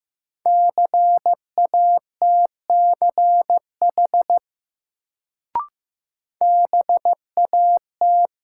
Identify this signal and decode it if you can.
Morse code, heard as a keyed tone, about 15 wpm, 700 Hz. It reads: CATCH BAT
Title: Sentences from Top 1500 Words - New Word Drill - 15wpm